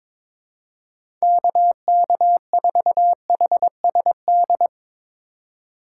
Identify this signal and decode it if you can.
Morse code, heard as a keyed tone, about 22 wpm, 700 Hz. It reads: KK4HSD
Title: USA Call Signs - 2x Character Spacing - 22wpm